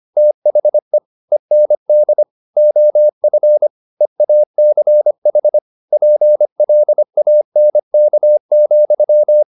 Morse code, heard as a keyed tone, about 25 wpm, 600 Hz. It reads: THE END OF EACH PLANK,
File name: the old man and the sea part 1 25wpm